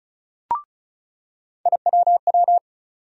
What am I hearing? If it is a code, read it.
Morse code, heard as a keyed tone, about 36 wpm, 700 Hz. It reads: IWW
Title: USA Call Sign Postfixes - 3 Letters 35wpm